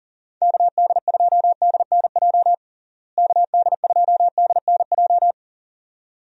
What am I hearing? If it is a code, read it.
Morse code, heard as a keyed tone, about 40 wpm, 700 Hz. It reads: KD2DNJ KD2DNJ
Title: USA Call Signs - 6 Characters - Repeated 2x - 40wpm